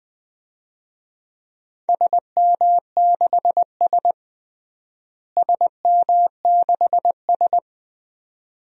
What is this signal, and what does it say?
Morse code, heard as a keyed tone, about 20 wpm, 700 Hz. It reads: SM6S SM6S
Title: DXCC-100 Call Signs - 4 Characters - Repeated 2x - 20wpm